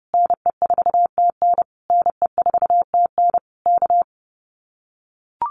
Morse code, heard as a keyed tone, about 30 wpm, 700 Hz. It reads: NE4TN NE4TN K
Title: Common QSO Phrases Encore 30wpm